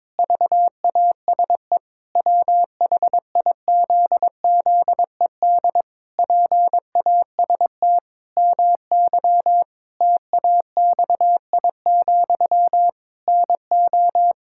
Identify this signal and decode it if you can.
Morse code, heard as a keyed tone, about 22 wpm, 700 Hz. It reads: VASE WHIZZED PAST MY TAXI, NO
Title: Pangram Challenge v2 - 22wpm